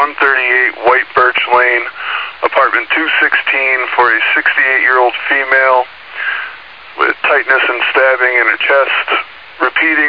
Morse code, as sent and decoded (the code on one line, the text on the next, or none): none